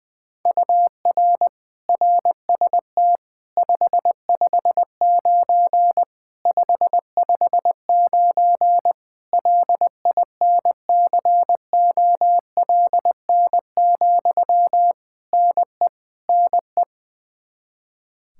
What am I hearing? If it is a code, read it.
Morse code, heard as a keyed tone, about 21 wpm, 700 Hz. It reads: UR RST 559 559 LINCOLN, NE NE
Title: Practice QSO - Exchange 1 - Beginning - 20wpm